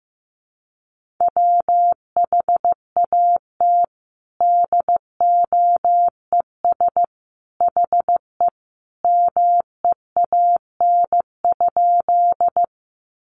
Morse code, heard as a keyed tone, about 15 wpm, 700 Hz. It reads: WHAT DOES HE MEAN?
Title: Sentences from Top 200 Words 15wpm